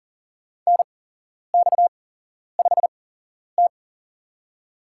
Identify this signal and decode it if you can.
Morse code, heard as a keyed tone, about 40 wpm, spaced as for 11 wpm, 700 Hz. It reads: NX5T
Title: USA Call Signs - 4 Characters - 8x Character Spacing - 40wpm